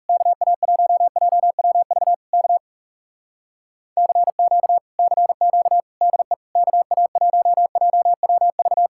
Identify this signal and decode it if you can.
Morse code, heard as a keyed tone, about 38 wpm, 700 Hz. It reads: KA1JWV K CQ CQ DE KA1JWV